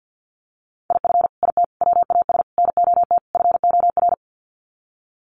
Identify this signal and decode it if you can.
Morse code, heard as a keyed tone, about 50 wpm, 700 Hz. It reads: IF IT WAS NOT FOR
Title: Sets of 5 Words Encore 50wpm